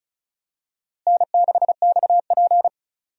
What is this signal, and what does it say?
Morse code, heard as a keyed tone, about 36 wpm, 700 Hz. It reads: N6XP
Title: USA Call Signs - 4 Characters - Repeated 2x - 35wpm